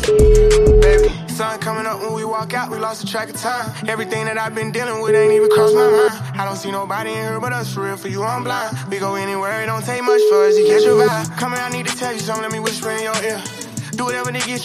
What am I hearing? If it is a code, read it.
Ringback tone (Europe, 425 Hz), with speech.